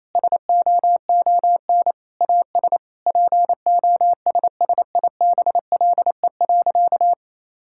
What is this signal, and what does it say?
Morse code, heard as a keyed tone, about 28 wpm, 700 Hz. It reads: SOON AS POSSIBLE.
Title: Sentences from Top 1000 Words - Review Part 5 - 28wpm